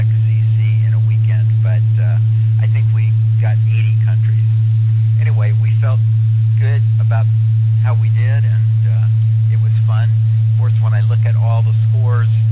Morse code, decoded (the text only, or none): none